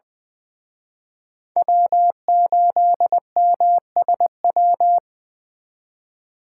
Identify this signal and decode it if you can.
Morse code, heard as a keyed tone, about 24 wpm, 700 Hz. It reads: W8MSW
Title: Speed Racing - USA Call Signs 20wpm